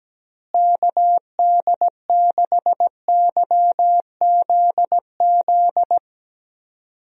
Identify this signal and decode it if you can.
Morse code, heard as a keyed tone, about 17 wpm, 700 Hz. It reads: KD6YZZ
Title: USA Call Signs - 6 Characters - Repeated 3x - 17wpm